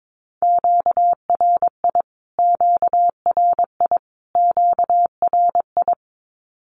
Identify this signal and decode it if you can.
Morse code, heard as a keyed tone, about 22 wpm, 700 Hz. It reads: QRI QRI QRI
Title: Q-Codes Repeated 3x 22wpm